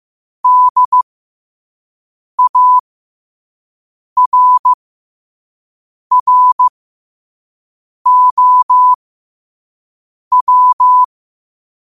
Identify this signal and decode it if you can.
Morse code, heard as a keyed tone, about 15 wpm, spaced as for 5 wpm, 1000 Hz. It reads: DARROW
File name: qotdfiles / 5wpm